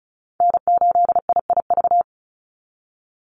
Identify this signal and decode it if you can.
Morse code, heard as a keyed tone, about 34 wpm, 700 Hz. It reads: N8IIV
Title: USA Call Signs 35wpm